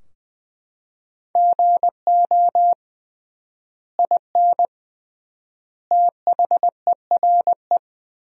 Morse code, heard as a keyed tone, about 21 wpm, 700 Hz. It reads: GO IN THERE
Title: 3x Word-Spacing - Sentences from Top 500 Words - 20wpm